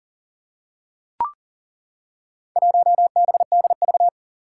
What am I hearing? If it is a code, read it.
Morse code, heard as a keyed tone, about 40 wpm, 700 Hz. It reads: 1BDV
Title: Four Challenging Letter-Number Combinations 40wpm